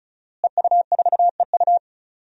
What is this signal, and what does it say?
Morse code, heard as a keyed tone, about 36 wpm, 700 Hz. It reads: EU4EU
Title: DXCC-100 Call Signs - 5 Characters - Repeated 2x - 35wpm